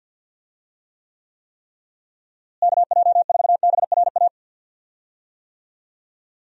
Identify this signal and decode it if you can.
Morse code, heard as a keyed tone, about 52 wpm, 700 Hz. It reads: KJ4BRA
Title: USA Call Signs - 6 Characters - Repeated 3x - 50wpm